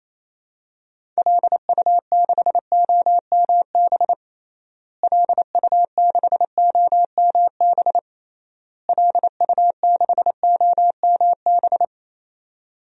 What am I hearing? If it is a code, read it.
Morse code, heard as a keyed tone, about 28 wpm, 700 Hz. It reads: LU6OMB LU6OMB LU6OMB
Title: DXCC-100 Call Signs - 6 Characters - Repeated 3x - 28wpm